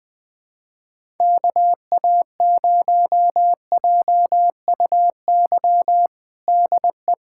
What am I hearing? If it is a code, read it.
Morse code, heard as a keyed tone, about 20 wpm, 700 Hz. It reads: KA0JUY DE
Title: USA Callsign DE Callsign 20wpm